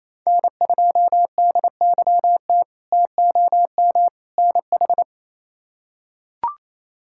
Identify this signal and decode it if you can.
Morse code, heard as a keyed tone, about 28 wpm, 700 Hz. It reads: N2DYT TOM NH
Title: K1USN SST QSO Practice - 28wpm